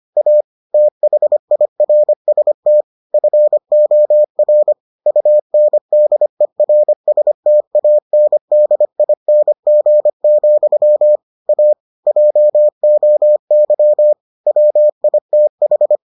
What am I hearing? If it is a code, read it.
Morse code, heard as a keyed tone, about 25 wpm, 600 Hz. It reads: A THIRST FOR UNDERSTANDING, A JOY WITH